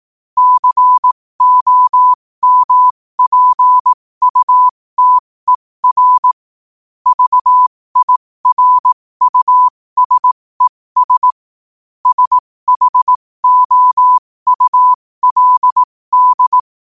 Morse code, heard as a keyed tone, about 18 wpm, spaced as for 15 wpm, 1000 Hz. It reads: COMPUTER VIRUSES SHOULD
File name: qotdfiles / 15wpm